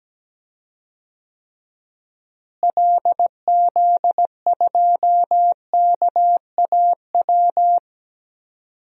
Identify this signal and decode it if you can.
Morse code, heard as a keyed tone, about 17 wpm, 700 Hz. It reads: LZ2KAW
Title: DXCC-100 Call Signs - 6 Characters - Repeated 2x - 17wpm